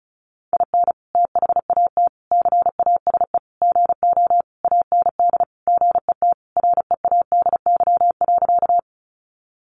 Morse code, heard as a keyed tone, about 34 wpm, 700 Hz. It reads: IN THAT CASE GO AND GET READY.
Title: Sentences from Top 700 Words 35wpm